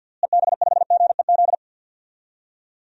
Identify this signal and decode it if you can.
Morse code, heard as a keyed tone, about 48 wpm, 700 Hz. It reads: EB5GEZ
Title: DXCC-100 Call Signs - 6 Characters - Repeated 2x - 50wpm